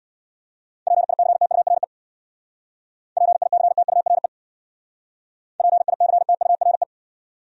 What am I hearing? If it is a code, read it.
Morse code, heard as a keyed tone, about 63 wpm, 700 Hz. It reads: PICTURE PICTURE PICTURE